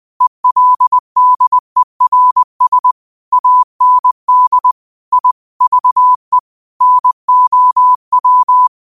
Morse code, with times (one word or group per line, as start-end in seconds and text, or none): 0.2-2.9 ELDERS
3.3-4.7 AND
5.1-5.3 I
5.6-6.4 VE
6.8-8.7 NOW